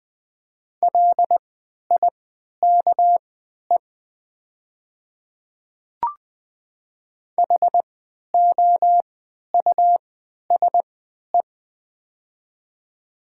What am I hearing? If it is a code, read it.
Morse code, heard as a keyed tone, about 20 wpm, spaced as for 11 wpm, 700 Hz. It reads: LIKE HOUSE
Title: 3x Character Spacing - Top 200 Words - 20wpm